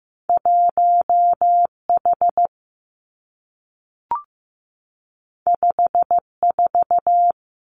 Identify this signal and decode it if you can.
Morse code, heard as a keyed tone, about 15 wpm, 700 Hz. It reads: 1 H 5 4
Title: Two Challenging Letter-Number Combinations 15wpm